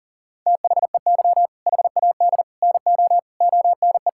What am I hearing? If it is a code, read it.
Morse code, heard as a keyed tone, about 37 wpm, 700 Hz. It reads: THEY HAD NO ONE
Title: Sentences from Top 1600 Words 40wpm